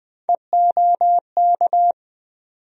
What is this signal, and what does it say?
Morse code, heard as a keyed tone, about 20 wpm, 700 Hz. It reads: EOK
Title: USA Call Sign Postfixes - 3 Letters 20wpm